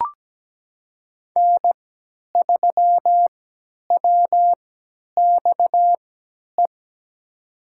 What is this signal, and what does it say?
Morse code, heard as a keyed tone, about 17 wpm, spaced as for 10 wpm, 700 Hz. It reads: N3WXE
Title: USA Call Signs - 5 Characters - 3x Character Spacing - 17wpm